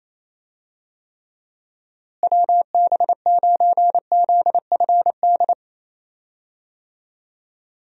Morse code, heard as a keyed tone, about 28 wpm, 700 Hz. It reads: WB9ZFD
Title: USA Call Signs - 6 Characters - Repeated 2x - 28wpm